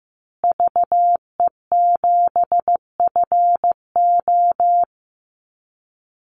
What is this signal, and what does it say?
Morse code, heard as a keyed tone, about 15 wpm, 700 Hz. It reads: VE7FO
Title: Call Signs Repeated 3x - 15wpm